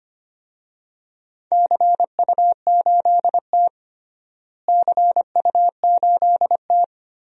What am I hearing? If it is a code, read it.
Morse code, heard as a keyed tone, about 25 wpm, 700 Hz. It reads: CU8T CU8T
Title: DXCC-100 Call Signs - 4 Characters - Repeated 3x - 25wpm